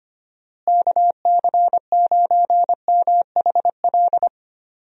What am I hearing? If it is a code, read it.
Morse code, heard as a keyed tone, about 25 wpm, 700 Hz. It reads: KC9MHL
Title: USA Call Signs - 6 Characters - Repeated 2x - 25wpm